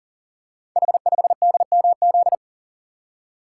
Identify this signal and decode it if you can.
Morse code, heard as a keyed tone, about 42 wpm, 700 Hz. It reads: H5DMZ
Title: Code Groups - 5 Letter:Numbers - 40wpm